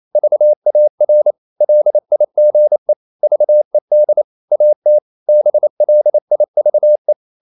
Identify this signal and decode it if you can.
Morse code, heard as a keyed tone, about 28 wpm, 600 Hz. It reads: VAR LIGE VED AT BLIVE
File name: Ferie QRQ 28wpm